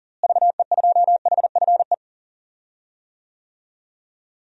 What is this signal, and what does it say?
Morse code, heard as a keyed tone, about 38 wpm, 700 Hz. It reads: VE2HFE
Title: DXCC-100 Call Signs - 6 Characters - Repeated 2x - 40wpm